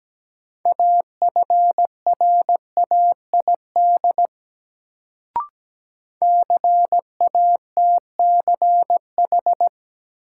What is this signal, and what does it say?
Morse code, heard as a keyed tone, about 17 wpm, 700 Hz. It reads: AFRAID CATCH
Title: Sentences from Top 1500 Words - New Word Drill Alt - 17wpm